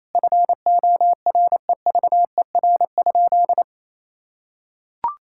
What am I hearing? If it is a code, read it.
Morse code, heard as a keyed tone, about 28 wpm, 700 Hz. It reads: FOREVER?
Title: Sentences from Top 1600 Words 28wpm